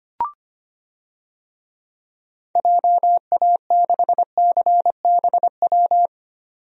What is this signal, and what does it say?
Morse code, heard as a keyed tone, about 25 wpm, 700 Hz. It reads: JA6CBW
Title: DXCC-100 Call Signs - 6 Characters - 25wpm